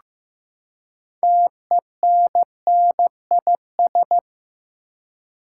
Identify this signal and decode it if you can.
Morse code, heard as a keyed tone, about 15 wpm, 700 Hz. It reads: TENNIS
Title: Six Letter Words - 15wpm